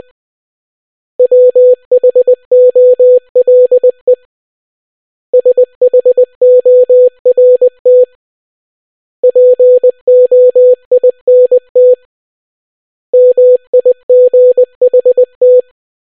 Morse code, heard as a keyed tone, about 20 wpm, 500 Hz. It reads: WHOLE SHORT POINT MIGHT